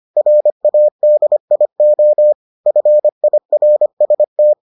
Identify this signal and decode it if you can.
Morse code, heard as a keyed tone, about 25 wpm, 600 Hz. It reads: RADIO FIRST